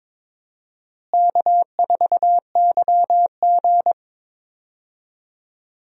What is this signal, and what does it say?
Morse code, heard as a keyed tone, about 22 wpm, 700 Hz. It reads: K4YG
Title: USA Call Signs - 4 Characters - Repeated 2x - 22wpm